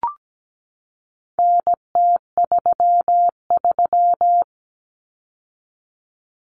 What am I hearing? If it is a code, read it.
Morse code, heard as a keyed tone, about 17 wpm, 700 Hz. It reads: NT33
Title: Common Cut Numbers - 4-Digit 17wpm